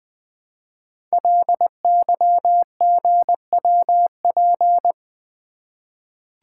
Morse code, heard as a keyed tone, about 21 wpm, 700 Hz. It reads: LYGWP
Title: Code Groups - 5 Letter:Numbers - 20wpm